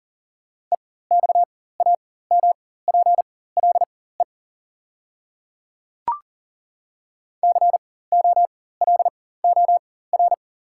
Morse code, heard as a keyed tone, about 41 wpm, spaced as for 19 wpm, 700 Hz. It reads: EXAMPLE COLOR
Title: 4x Character Spacing - Top 300 Words - 40wpm